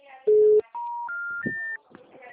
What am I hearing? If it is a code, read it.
Special information tone.
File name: out- -2015- - - .35979